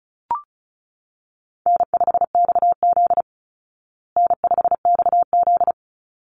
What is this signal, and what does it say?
Morse code, heard as a keyed tone, about 34 wpm, 700 Hz. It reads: N5XZ N5XZ
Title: Call Signs Repeated 3x - 35wpm